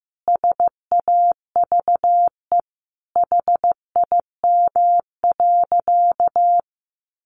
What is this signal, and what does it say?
Morse code, heard as a keyed tone, about 15 wpm, 700 Hz. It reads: SAVE HIM.
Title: Sentences from Top 1100 Words 15wpm